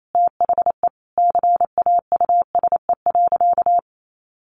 Morse code, heard as a keyed tone, about 28 wpm, 700 Hz. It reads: THE CAUSE.
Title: Sentences from Top 700 Words 28wpm